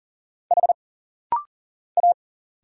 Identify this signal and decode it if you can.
Morse code, heard as a keyed tone, about 40 wpm, 700 Hz. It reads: H A
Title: Single Letters - Rapid-Fire 40wpm